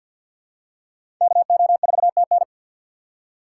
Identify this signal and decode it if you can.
Morse code, heard as a keyed tone, about 50 wpm, 700 Hz. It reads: KO4TN
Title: USA Call Signs - 6x Character Spacing - 50wpm